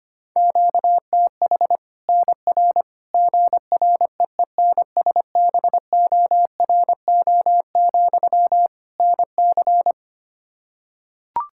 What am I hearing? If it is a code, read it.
Morse code, heard as a keyed tone, about 25 wpm, 700 Hz. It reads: QTH NR GREENSBORO, NC
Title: Practice QSO - Exchange 1 - Beginning - 25wpm